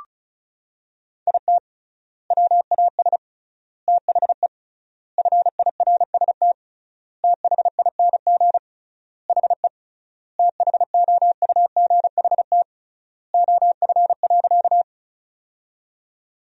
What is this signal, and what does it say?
Morse code, heard as a keyed tone, about 36 wpm, 700 Hz. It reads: IT WAS THE FIRST THING HE THOUGHT OF.
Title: 3x Word-Spacing - Sentences from Top 300 Words - 35wpm